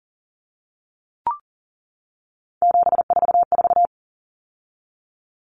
Morse code, heard as a keyed tone, about 40 wpm, 700 Hz. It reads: 744
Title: Numbers - 3-Digit 40wpm